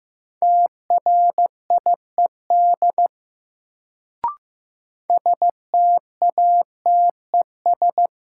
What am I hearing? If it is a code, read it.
Morse code, heard as a keyed tone, about 16 wpm, 700 Hz. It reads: TRIED STATES
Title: Sets of 2 Words - New Word Drill - 15wpm